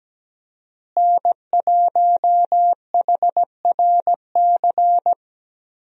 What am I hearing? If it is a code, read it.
Morse code, heard as a keyed tone, about 17 wpm, 700 Hz. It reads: N1HRC
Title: USA Call Signs - 5 Characters - Repeated 3x - 17wpm